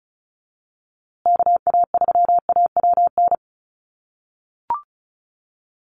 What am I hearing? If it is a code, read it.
Morse code, heard as a keyed tone, about 34 wpm, 700 Hz. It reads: KA3AWN
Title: USA Call Signs Repeated 2x 35wpm